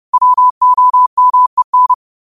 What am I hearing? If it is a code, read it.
Morse code, heard as a keyed tone, about 30 wpm, 1000 Hz. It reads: WOMEN